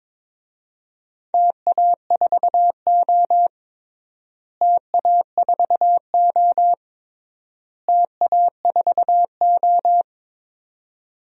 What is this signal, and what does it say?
Morse code, heard as a keyed tone, about 22 wpm, 700 Hz. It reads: TA4O TA4O TA4O
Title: DXCC-100 Call Signs - 4 Characters - Repeated 3x - 22wpm